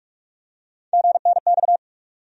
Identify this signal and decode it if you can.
Morse code, heard as a keyed tone, about 42 wpm, 700 Hz. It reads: GNX
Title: USA Call Sign Postfixes - 3 Letters 45wpm